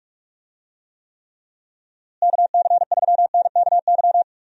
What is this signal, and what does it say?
Morse code, heard as a keyed tone, about 41 wpm, 700 Hz. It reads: KC3NKY
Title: USA Call Signs - 6 Characters - Repeated 3x - 45wpm